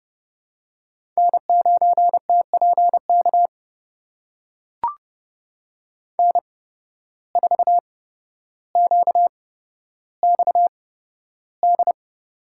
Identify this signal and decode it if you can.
Morse code, heard as a keyed tone, about 32 wpm, spaced as for 13 wpm, 700 Hz. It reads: N9TPK N4QXD
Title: USA Call Signs - 5 Characters - 8x Character Spacing - 30wpm